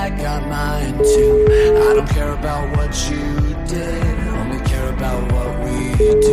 Ringback tone (Europe, 425 Hz), with noise.